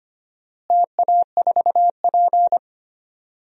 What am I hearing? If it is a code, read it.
Morse code, heard as a keyed tone, about 25 wpm, 700 Hz. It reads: TA4P
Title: DXCC-100 Call Signs - 4 Characters - Repeated 3x - 25wpm